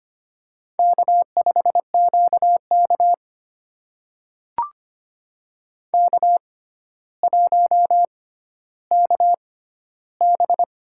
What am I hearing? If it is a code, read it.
Morse code, heard as a keyed tone, about 25 wpm, spaced as for 13 wpm, 700 Hz. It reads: K5QK K1KB
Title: USA Call Signs - 4 Characters - 6x Character Spacing - 25wpm